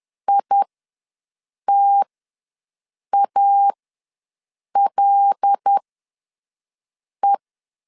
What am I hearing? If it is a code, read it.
Morse code, heard as a keyed tone, about 10 wpm, spaced as for 5 wpm, 800 Hz. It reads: ITALE
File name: LEZIONI / 5 LBVC / 5 L ws 10wpm 3636